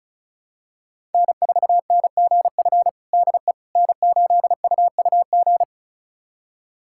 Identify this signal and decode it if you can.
Morse code, heard as a keyed tone, about 34 wpm, 700 Hz. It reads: N4NGF DE N8UUG
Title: USA Callsign DE Callsign - 5 Characters - 35wpm